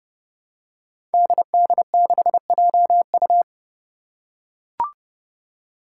Morse code, heard as a keyed tone, about 30 wpm, 700 Hz. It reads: DD6JU
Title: Five Challenging Letter-Number Combinations 30wpm